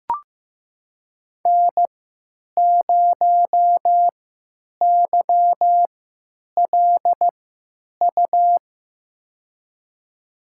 Morse code, heard as a keyed tone, about 15 wpm, spaced as for 8 wpm, 700 Hz. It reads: N0YLU
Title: USA Call Signs - 5 Characters - 3x Character Spacing - 15wpm